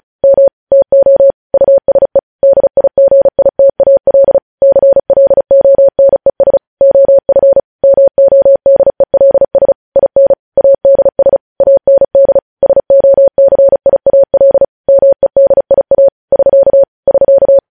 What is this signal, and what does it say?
Morse code, heard as a keyed tone, about 36 wpm, 550 Hz. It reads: M TO USE DIGITAL CLONES OF MODELS IN ADS AND SOCIAL MEDIA <SK> <SK>
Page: News headlines in Morse code at 35 wpm